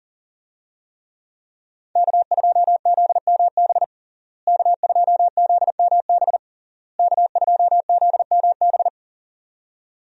Morse code, heard as a keyed tone, about 40 wpm, 700 Hz. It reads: K2ZMB K2ZMB K2ZMB
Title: USA Call Signs - 5 Characters - Repeated 3x - 40wpm